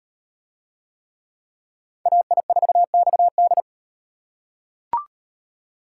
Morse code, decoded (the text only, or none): AI4XD